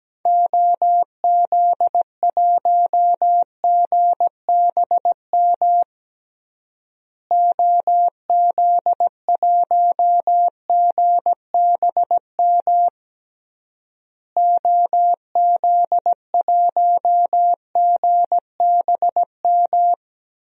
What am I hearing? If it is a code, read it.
Morse code, heard as a keyed tone, about 17 wpm, 700 Hz. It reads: OZ1GBM OZ1GBM OZ1GBM
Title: DXCC-100 Call Signs - 6 Characters - Repeated 3x - 17wpm